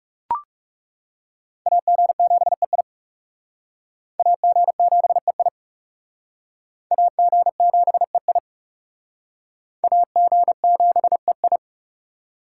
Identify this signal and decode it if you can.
Morse code, heard as a keyed tone, about 40 wpm, 700 Hz. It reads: AG7EI AG7EI AG7EI AG7EI